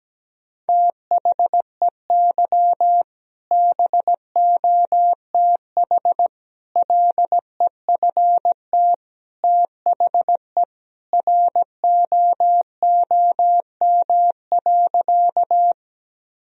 Morse code, heard as a keyed tone, about 17 wpm, 700 Hz. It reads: THEY BOTH LEFT THE ROOM.
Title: Sentences from Top 1000 Words - Review Part 1 - 17wpm